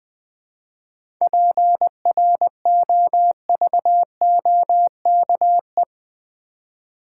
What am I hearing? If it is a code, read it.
Morse code, heard as a keyed tone, about 20 wpm, 700 Hz. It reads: PROVOKE
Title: 7 Letter Words - First 3 Letters Given - 20wpm